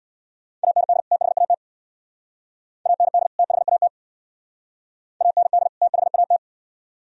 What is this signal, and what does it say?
Morse code, heard as a keyed tone, about 63 wpm, 700 Hz. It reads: AND THAT AND THAT AND THAT